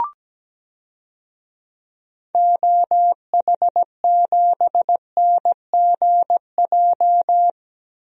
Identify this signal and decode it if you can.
Morse code, heard as a keyed tone, about 17 wpm, 700 Hz. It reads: OH7NGJ
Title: DXCC-100 Call Signs - 6 Characters - Repeated 3x - 17wpm